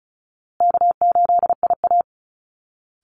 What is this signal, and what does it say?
Morse code, heard as a keyed tone, about 34 wpm, 700 Hz. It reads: K8IA
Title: Call Signs Repeated 3x - 35wpm